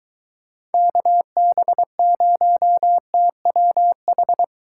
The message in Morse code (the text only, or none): KB0TWH